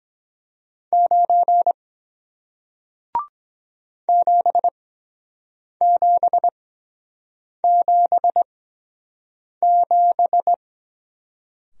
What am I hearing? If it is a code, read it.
Morse code, heard as a keyed tone, about 23 wpm, spaced as for 7 wpm, 700 Hz. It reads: 9 7777